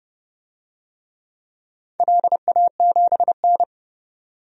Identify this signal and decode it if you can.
Morse code, heard as a keyed tone, about 28 wpm, 700 Hz. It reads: LA7N
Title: DXCC-100 Call Signs - 4 Characters - 30wpm